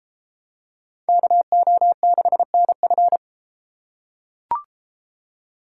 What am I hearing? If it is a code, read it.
Morse code, heard as a keyed tone, about 32 wpm, 700 Hz. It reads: KO6NF